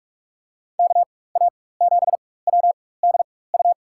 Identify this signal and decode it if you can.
Morse code, heard as a keyed tone, about 45 wpm, spaced as for 21 wpm, 700 Hz. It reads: KA7WDU
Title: USA Call Signs - 6 Characters - 4x Character Spacing - 45wpm